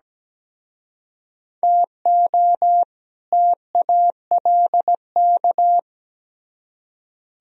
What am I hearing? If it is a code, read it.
Morse code, heard as a keyed tone, about 20 wpm, 700 Hz. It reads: TO TALK
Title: Speed-Racing - Sets of 2 Words 17wpm